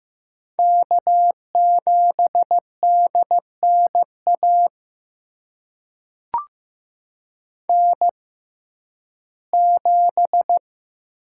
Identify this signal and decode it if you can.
Morse code, heard as a keyed tone, about 15 wpm, spaced as for 11 wpm, 700 Hz. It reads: K7DNA N7
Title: USA Call Signs - 5 Characters - 6x Character Spacing - 15wpm